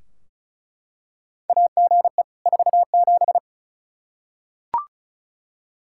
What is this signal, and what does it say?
Morse code, heard as a keyed tone, about 36 wpm, 700 Hz. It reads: AGE 47
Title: Practice QSO - Exchange 3 - More Info - 35wpm